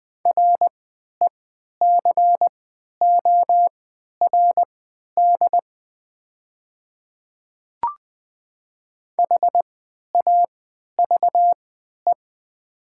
Morse code, heard as a keyed tone, about 20 wpm, spaced as for 11 wpm, 700 Hz. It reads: RECORD HAVE
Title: 3x Character Spacing - Top 500 Words - 20wpm